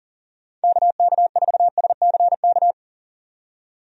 Morse code, heard as a keyed tone, about 40 wpm, 700 Hz. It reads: KK4SCK
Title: USA Call Signs - 6 Characters - 40wpm